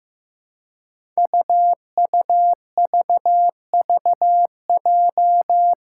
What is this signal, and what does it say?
Morse code, heard as a keyed tone, about 15 wpm, 700 Hz. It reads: UUVVJ
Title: Five Challenging Letter-Number Combinations 15wpm